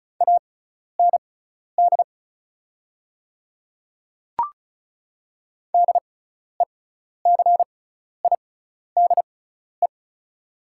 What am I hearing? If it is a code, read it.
Morse code, heard as a keyed tone, about 36 wpm, spaced as for 12 wpm, 700 Hz. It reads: AND DECIDE
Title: 6x Character Spacing - Top 500 Words - 35wpm